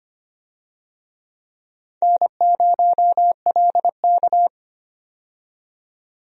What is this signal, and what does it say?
Morse code, heard as a keyed tone, about 25 wpm, 700 Hz. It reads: N0LK
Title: USA Call Signs - 4 Characters - 25wpm